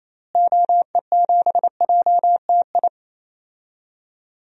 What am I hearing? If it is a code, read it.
Morse code, heard as a keyed tone, about 27 wpm, 700 Hz. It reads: OE7JTI
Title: DXCC-100 Call Signs - 6 Characters - Repeated 3x - 28wpm